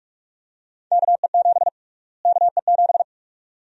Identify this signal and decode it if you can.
Morse code, heard as a keyed tone, about 46 wpm, 700 Hz. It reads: KE7 KE7
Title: USA Call Sign Prefixes - Repeated 2x - 45wpm